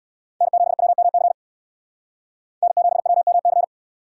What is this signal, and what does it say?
Morse code, heard as a keyed tone, about 61 wpm, 700 Hz. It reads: N7WMC N7WMC
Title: Speed Racing - USA 1x3 Call Signs 50wpm